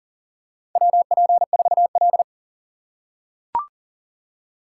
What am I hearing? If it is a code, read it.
Morse code, heard as a keyed tone, about 40 wpm, 700 Hz. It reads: WP4L
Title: USA Call Signs - 4 Characters - 6x Character Spacing - 40wpm